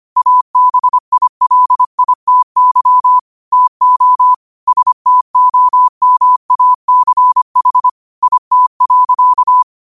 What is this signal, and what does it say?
Morse code, heard as a keyed tone, about 25 wpm, 1000 Hz. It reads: ABILITY TO STOMACH IT.
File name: qotdfiles / 25wpm